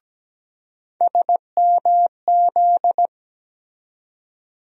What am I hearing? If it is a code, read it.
Morse code, heard as a keyed tone, about 17 wpm, 700 Hz. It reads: SMZ
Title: USA Call Sign Postfixes - 3 Letters 17wpm